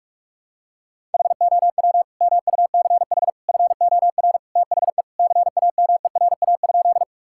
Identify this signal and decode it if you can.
Morse code, heard as a keyed tone, about 41 wpm, 700 Hz. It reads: HOW MUCH FOR THE CAMERA?
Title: Sentences from Top 1000 Words - Review Part 5 - 45wpm